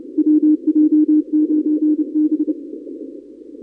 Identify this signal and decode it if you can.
Morse code, heard as a keyed tone, about 29 wpm, 310 Hz. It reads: WJ9B
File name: contests / 2019-cw-wpx-kt5j-k5pi- -2330 0-31